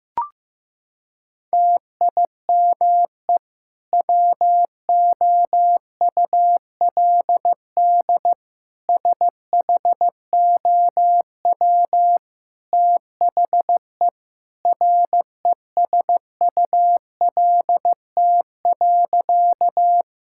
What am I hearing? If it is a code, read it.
Morse code, heard as a keyed tone, about 15 wpm, 700 Hz. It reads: TIME WOULD SHOW THE RESULT.
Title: Sentences from Top 1000 Words - Review Part 5 - 15wpm